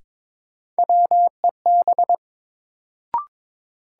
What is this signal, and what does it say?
Morse code, heard as a keyed tone, about 22 wpm, 700 Hz. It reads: WEB
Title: Sentences from Top 1000 Words - Review Part 4 - New Word Drill - 22wpm